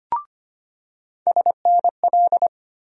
Morse code, heard as a keyed tone, about 25 wpm, 700 Hz. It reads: SNL
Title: USA Call Sign Postfixes - 3 Letters 25wpm